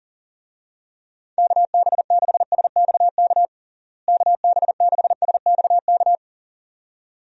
Morse code, heard as a keyed tone, about 40 wpm, 700 Hz. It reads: KB6SXK KB6SXK
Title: USA Call Signs - 6 Characters - Repeated 2x - 40wpm